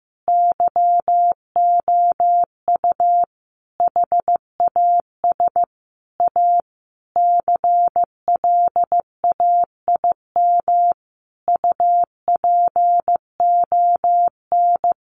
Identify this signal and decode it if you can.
Morse code, heard as a keyed tone, about 15 wpm, 700 Hz. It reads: YOU HAS A CLAIM UPON
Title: Sentences from Top 1500 Words 15wpm